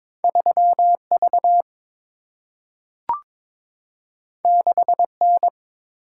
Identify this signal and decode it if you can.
Morse code, heard as a keyed tone, about 22 wpm, 700 Hz. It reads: 3 V 6 N
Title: Single-Letter Single-Number 22wpm